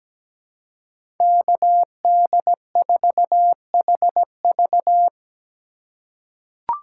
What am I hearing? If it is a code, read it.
Morse code, heard as a keyed tone, about 17 wpm, 700 Hz. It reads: KD4HV
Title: USA Call Signs - 5 Characters - 4x Character Spacing - 17wpm